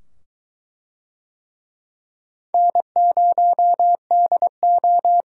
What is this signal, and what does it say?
Morse code, heard as a keyed tone, about 22 wpm, 700 Hz. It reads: N0DO